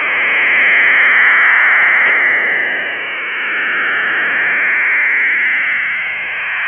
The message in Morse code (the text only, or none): none